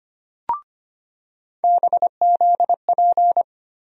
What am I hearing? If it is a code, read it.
Morse code, heard as a keyed tone, about 25 wpm, 700 Hz. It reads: BZP
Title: USA Call Sign Postfixes - 3 Letters 25wpm